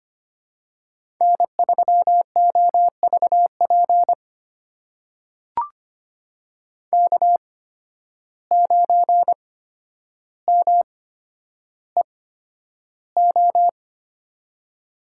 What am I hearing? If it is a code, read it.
Morse code, heard as a keyed tone, about 25 wpm, spaced as for 11 wpm, 700 Hz. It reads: N3OVP K9MEO
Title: USA Call Signs - 5 Characters - 8x Character Spacing - 25wpm